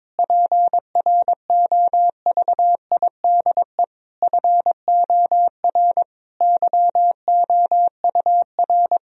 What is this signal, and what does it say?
Morse code, heard as a keyed tone, about 22 wpm, 700 Hz. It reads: PROVIDE FOR YOUR
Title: Sentences from Top 1000 Words - Review Part 3 - 22wpm